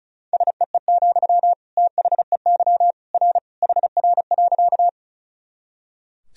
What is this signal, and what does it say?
Morse code, heard as a keyed tone, about 36 wpm, 700 Hz. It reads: SEE, THEY R HR.
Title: Sentences from Top 200 Words and Abbreviations 35wpm